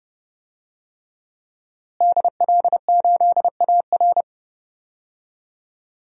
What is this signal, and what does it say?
Morse code, heard as a keyed tone, about 30 wpm, 700 Hz. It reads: DL8AR
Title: DXCC-100 Call Signs - 5 Characters - Repeated 3x - 30wpm